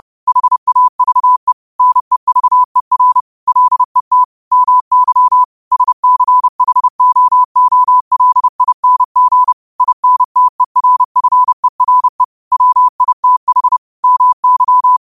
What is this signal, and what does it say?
Morse code, heard as a keyed tone, about 30 wpm, 1000 Hz. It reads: HAVE NEVER LET MY SCHOOLING INTERFERE WITH MY